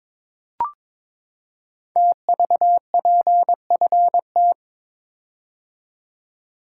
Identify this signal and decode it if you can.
Morse code, heard as a keyed tone, about 22 wpm, 700 Hz. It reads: TVPFT
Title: Code Groups - 5 Letters - 22wpm